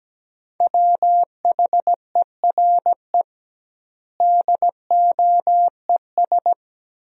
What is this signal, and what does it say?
Morse code, heard as a keyed tone, about 17 wpm, 700 Hz. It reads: WHERE DOES
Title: 2x Word-Spacing - Sentences from Top 200 Words - 17wpm